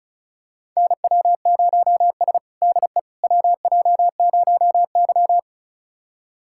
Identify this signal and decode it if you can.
Morse code, heard as a keyed tone, about 34 wpm, 700 Hz. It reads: NW0S DE WJ0Y
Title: USA Callsign DE Callsign - 4 Characters - 35wpm